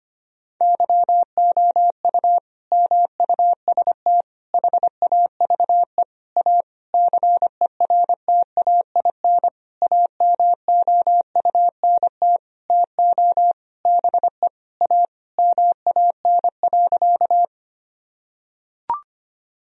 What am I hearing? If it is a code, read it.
Morse code, heard as a keyed tone, about 25 wpm, 700 Hz. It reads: YOU MUST HAVE A CERTAIN AMOUNT TO BE A MAN.
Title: Sentences from Top 1000 Words - Review Part 4 - 25wpm